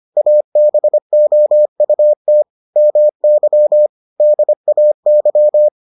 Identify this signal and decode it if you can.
Morse code, heard as a keyed tone, about 25 wpm, 600 Hz. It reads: ABOUT MY DAY